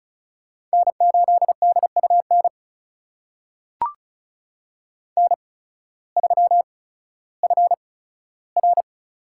Morse code, heard as a keyed tone, about 34 wpm, spaced as for 16 wpm, 700 Hz. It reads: N8DUN N3FR
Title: USA Call Signs - 5 Characters - 8x Character Spacing - 35wpm